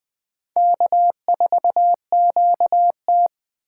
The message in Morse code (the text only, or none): K4QT